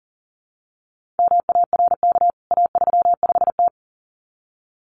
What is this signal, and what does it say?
Morse code, heard as a keyed tone, about 40 wpm, 700 Hz. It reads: MARK A35T
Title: CWT Exchange 40wpm